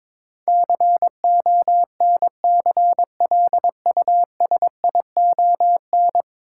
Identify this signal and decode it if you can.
Morse code, heard as a keyed tone, about 22 wpm, 700 Hz. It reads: CONCLUSION
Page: morse-code-ninja-practice-22wpm